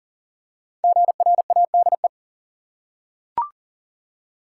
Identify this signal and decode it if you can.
Morse code, heard as a keyed tone, about 40 wpm, 700 Hz. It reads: GRADE
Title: Five Letter Words - 40wpm